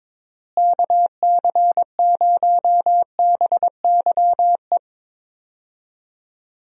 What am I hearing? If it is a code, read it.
Morse code, heard as a keyed tone, about 22 wpm, 700 Hz. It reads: KC0BYE
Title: USA Call Signs - 6 Characters - Repeated 2x - 22wpm